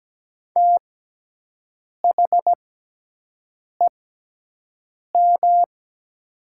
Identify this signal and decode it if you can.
Morse code, heard as a keyed tone, about 17 wpm, spaced as for 6 wpm, 700 Hz. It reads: THEM
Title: 6x Character Spacing - Top 100 Words - 17wpm